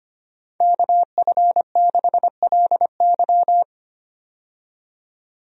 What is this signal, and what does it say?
Morse code, heard as a keyed tone, about 25 wpm, 700 Hz. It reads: KF6LY
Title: USA Call Signs - 5 Characters - 25wpm